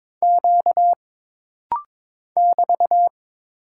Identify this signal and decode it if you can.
Morse code, heard as a keyed tone, about 22 wpm, 700 Hz. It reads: Q <BT>
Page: This rapid-fire timing is geared to push your ICR (Instant Character Recognition) skills.